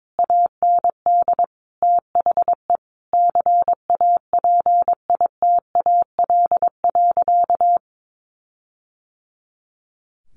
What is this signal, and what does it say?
Morse code, heard as a keyed tone, about 22 wpm, 700 Hz. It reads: AND THE CAPITAL.
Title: Sentences from Top 1500 Words 22wpm